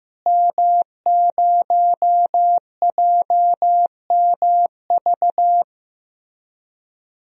Morse code, heard as a keyed tone, about 15 wpm, 700 Hz. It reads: M0JMV
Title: DXCC-100 Call Signs - 5 Characters - Repeated 3x - 15wpm